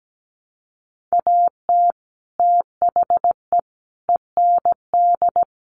Morse code, heard as a keyed tone, about 17 wpm, 700 Hz. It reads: AT THE END
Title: Sets of 4 Words 17wpm